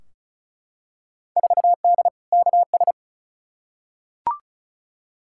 Morse code, heard as a keyed tone, about 36 wpm, 700 Hz. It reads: 4D KS
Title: Field Day Exchanges Encore - 35wpm